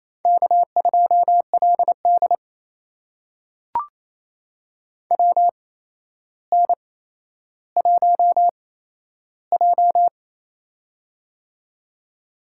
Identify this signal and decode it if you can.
Morse code, heard as a keyed tone, about 28 wpm, spaced as for 12 wpm, 700 Hz. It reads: K2LD WN1J
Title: USA Call Signs - 8x Character Spacing - 28wpm